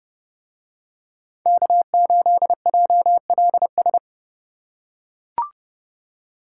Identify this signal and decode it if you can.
Morse code, heard as a keyed tone, about 29 wpm, 700 Hz. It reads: K8JLS